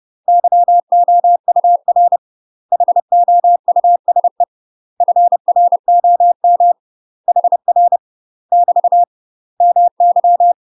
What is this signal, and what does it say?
Morse code, heard as a keyed tone, about 30 wpm, 700 Hz. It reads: YOUR HOUSE FROM HR <BT> MY
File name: UK MASTER QSOs TRAINING / 30WPM / UK MASTER QSOs 0046